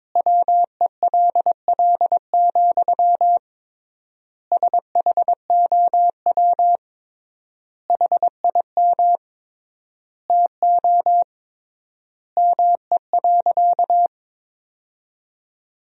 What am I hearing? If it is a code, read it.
Morse code, heard as a keyed tone, about 22 wpm, 700 Hz. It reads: WELL, SHOW HIM TO ME.
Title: 3x Word-Spacing - Sentences from Top 200 Words - 22wpm